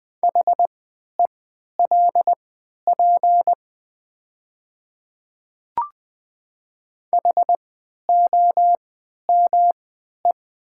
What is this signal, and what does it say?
Morse code, heard as a keyed tone, about 20 wpm, spaced as for 11 wpm, 700 Hz. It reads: HELP HOME
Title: 3x Character Spacing - Top 300 Words - 20wpm